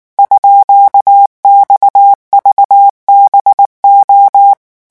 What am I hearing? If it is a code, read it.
Morse code, heard as a keyed tone, about 19 wpm, 800 Hz. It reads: _XVBO
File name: morse mp3 / revision12-vitesse 18 mots